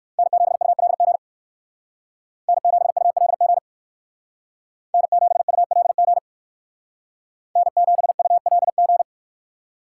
Morse code, heard as a keyed tone, about 58 wpm, 700 Hz. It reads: N7ULG N7ULG N7ULG N7ULG